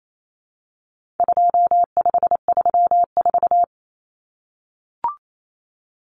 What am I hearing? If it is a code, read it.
Morse code, heard as a keyed tone, about 28 wpm, 700 Hz. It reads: 2534
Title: Numbers - 4-Digit 28wpm